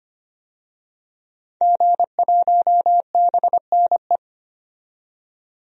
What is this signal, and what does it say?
Morse code, heard as a keyed tone, about 25 wpm, 700 Hz. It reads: G1BNE
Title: DXCC-100 Call Signs - 5 Characters - 25wpm